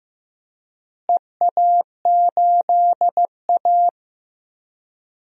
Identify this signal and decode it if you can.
Morse code, heard as a keyed tone, about 15 wpm, 700 Hz. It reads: EA8A
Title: DXCC-100 Call Signs - 4 Characters - Repeated 3x - 15wpm